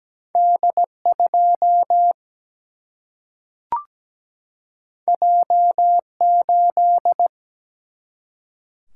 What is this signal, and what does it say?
Morse code, heard as a keyed tone, about 17 wpm, 700 Hz. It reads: D 2 J 8
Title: Single-Letter Single-Number 17wpm